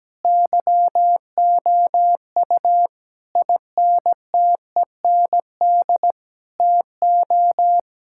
Morse code, heard as a keyed tone, about 17 wpm, 700 Hz. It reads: YOU INTEND TO
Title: Sentences from Top 1600 Words 17wpm